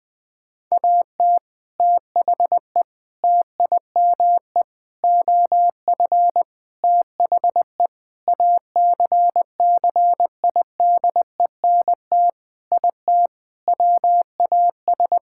Morse code, heard as a keyed tone, about 20 wpm, 700 Hz. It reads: AT THE TIME OF THE ACCIDENT IT WAS
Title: Sentences with 5-Word Phrases - Spoken Beginning - 20wpm